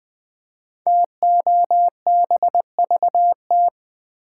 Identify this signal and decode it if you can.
Morse code, heard as a keyed tone, about 20 wpm, 700 Hz. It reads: TOBVT
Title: Code Groups - 5 Letters - 20wpm